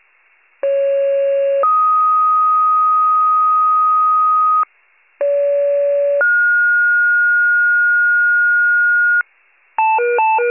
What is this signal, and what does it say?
Two-tone page, heard as a one-second tone, then a three-second tone, about 550 then 1250 Hz, about 570 then 1430 Hz, with noise. It is heard from 0.6 to 9.2 s.